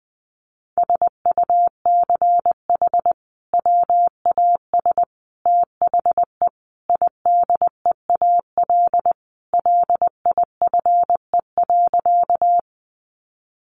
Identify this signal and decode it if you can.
Morse code, heard as a keyed tone, about 19 wpm, 700 Hz. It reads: SUCH WAS THE IDEAL LIFE.
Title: Sentences from Top 1400 Words 20wpm